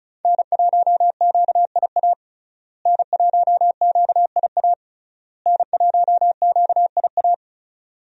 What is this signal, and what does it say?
Morse code, heard as a keyed tone, about 34 wpm, 700 Hz. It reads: N1QIA N1QIA N1QIA
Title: USA Call Signs - 5 Characters - Repeated 3x - 35wpm